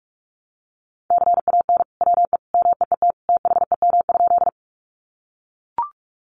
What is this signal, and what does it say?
Morse code, heard as a keyed tone, about 44 wpm, 700 Hz. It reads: CAN WE MEET THEM?
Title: Sentences from Top 1100 Words 45wpm